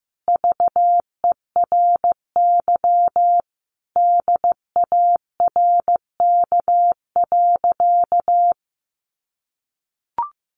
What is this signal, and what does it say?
Morse code, heard as a keyed tone, about 15 wpm, 700 Hz. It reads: VERY DARK.
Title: Sentences from Top 500 Words 15wpm